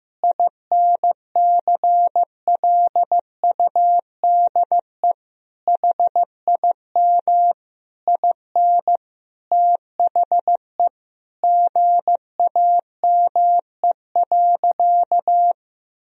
Morse code, heard as a keyed tone, about 15 wpm, 700 Hz. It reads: INCLUDE HIM IN THE GAME.
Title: Sentences from Top 1000 Words - Review Part 3 - 15wpm